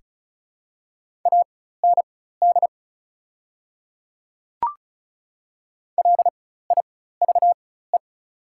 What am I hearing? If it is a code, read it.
Morse code, heard as a keyed tone, about 34 wpm, spaced as for 16 wpm, 700 Hz. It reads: AND LIVE
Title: 4x Character Spacing - Top 300 Words - 35wpm